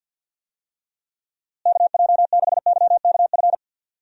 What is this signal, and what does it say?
Morse code, heard as a keyed tone, about 53 wpm, 700 Hz. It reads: KJ6YKF
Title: USA Call Signs - 6 Characters - Repeated 2x - 50wpm